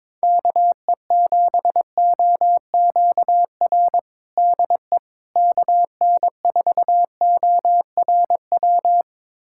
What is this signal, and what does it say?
Morse code, heard as a keyed tone, about 22 wpm, 700 Hz. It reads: KE7OQR DE KN4ORW
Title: USA Callsign DE Callsign - 6 Characters - 22wpm